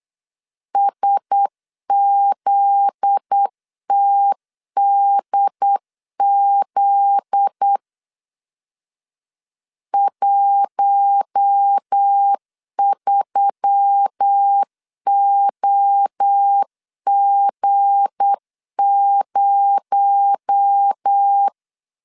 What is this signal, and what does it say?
Morse code, heard as a keyed tone, about 8 wpm, 800 Hz. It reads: SZTDZ 13OG0
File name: LEZIONI / 7 GF9Z / 7 GF9Z 8wpm 3213